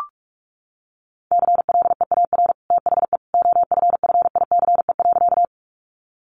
Morse code, heard as a keyed tone, about 44 wpm, 700 Hz. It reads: CLEAR THE OFFICE.
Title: Sentences from Top 1000 Words 45wpm